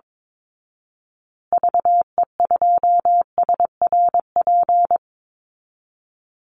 Morse code, heard as a keyed tone, about 22 wpm, 700 Hz. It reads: VE2SRP
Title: Call Signs Repeated 3x - 22wpm